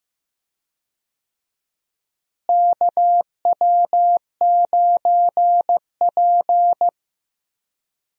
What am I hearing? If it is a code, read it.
Morse code, heard as a keyed tone, about 15 wpm, 700 Hz. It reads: KW9P
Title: USA Call Signs - 4 Characters - Repeated 3x - 15wpm